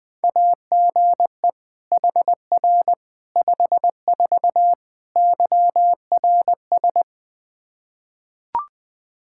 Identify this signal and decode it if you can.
Morse code, heard as a keyed tone, about 21 wpm, 700 Hz. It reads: AGE HR 54 YRS
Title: Practice QSO - Exchange 3 - More Info - 20wpm